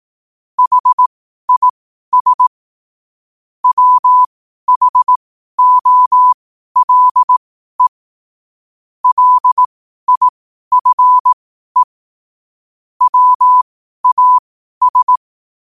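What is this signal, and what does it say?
Morse code, heard as a keyed tone, about 18 wpm, spaced as for 12 wpm, 1000 Hz. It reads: HIS WHOLE LIFE WAS
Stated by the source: Morse code sent at 12 words per minute.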